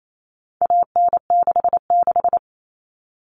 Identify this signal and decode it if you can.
Morse code, heard as a keyed tone, about 28 wpm, 700 Hz. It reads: AN66
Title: All Cut Numbers - 4-Digit 28wpm